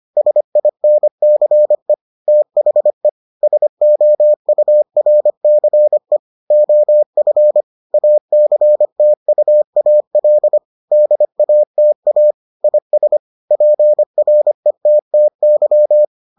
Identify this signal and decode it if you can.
Morse code, heard as a keyed tone, about 25 wpm, 600 Hz. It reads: SINCE THE SOURCE OF ACTUAL DATA IS PRETTY